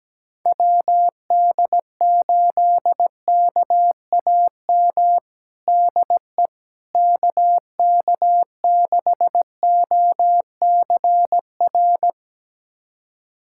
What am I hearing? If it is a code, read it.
Morse code, heard as a keyed tone, about 17 wpm, 700 Hz. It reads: WD8KAM DE KK6OCR
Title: USA Callsign DE Callsign 17wpm